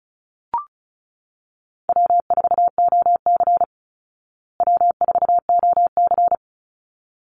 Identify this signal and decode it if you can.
Morse code, heard as a keyed tone, about 34 wpm, 700 Hz. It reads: W4OC W4OC
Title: Easy Call Signs Repeated 2x - 35wpm